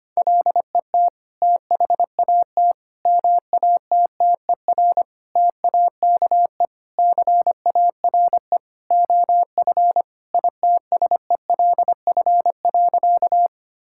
Morse code, heard as a keyed tone, about 25 wpm, 700 Hz. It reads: LET THAT MATTER TAKE CARE OF ITSELF.
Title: Sentences from Top 1000 Words - Review Part 5 - 25wpm